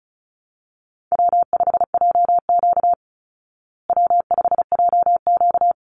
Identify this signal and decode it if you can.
Morse code, heard as a keyed tone, about 34 wpm, 700 Hz. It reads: W5JQ W5JQ
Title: Easy Call Signs Repeated 2x - 35wpm